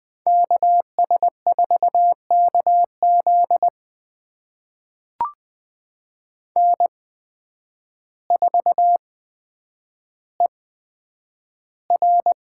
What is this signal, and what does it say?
Morse code, heard as a keyed tone, about 20 wpm, spaced as for 9 wpm, 700 Hz. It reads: KS4KZ N4ER
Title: USA Call Signs - 8x Character Spacing - 20wpm